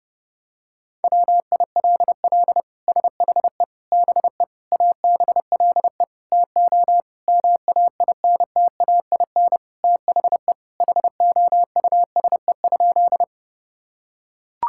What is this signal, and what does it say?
Morse code, heard as a keyed tone, about 30 wpm, 700 Hz. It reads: WILL SHE BE ABLE TO MAINTAIN THE HOUSE?
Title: Sentences from Top 1000 Words - Review Part 5 - 30wpm